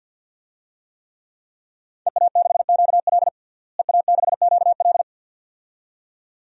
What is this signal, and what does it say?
Morse code, heard as a keyed tone, about 46 wpm, 700 Hz. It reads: EA6QL EA6QL
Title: DXCC-100 Call Signs - 5 Characters - Repeated 2x - 50wpm